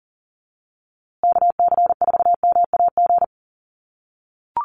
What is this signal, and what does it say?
Morse code, heard as a keyed tone, about 40 wpm, 700 Hz. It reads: KC4MAG
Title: USA Call Signs Repeated 3x 40wpm